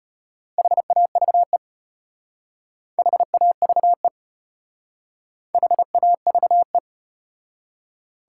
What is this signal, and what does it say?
Morse code, heard as a keyed tone, about 34 wpm, 700 Hz. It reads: HAVE HAVE HAVE